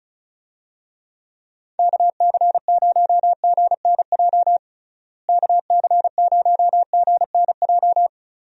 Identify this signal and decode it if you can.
Morse code, heard as a keyed tone, about 34 wpm, 700 Hz. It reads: KC0GNJ KC0GNJ
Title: USA Call Signs - 6 Characters - Repeated 3x - 35wpm